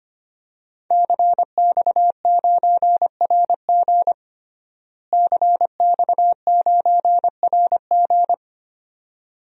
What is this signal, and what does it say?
Morse code, heard as a keyed tone, about 25 wpm, 700 Hz. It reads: CX9RG CX9RG
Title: DXCC-100 Call Signs - 5 Characters - Repeated 2x - 25wpm